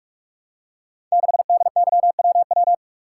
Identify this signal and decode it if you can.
Morse code, heard as a keyed tone, about 44 wpm, 690 Hz. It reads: 6DYWW
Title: Code Groups - 5 Letter:Numbers - 45wpm